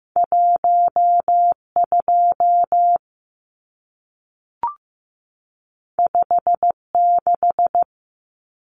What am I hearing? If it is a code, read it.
Morse code, heard as a keyed tone, about 15 wpm, 700 Hz. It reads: 12 56
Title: Numbers - 2-Digit 15wpm